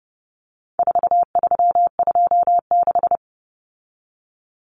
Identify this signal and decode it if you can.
Morse code, heard as a keyed tone, about 30 wpm, 700 Hz. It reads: 4326
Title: Common Cut Numbers - 4-Digit 30wpm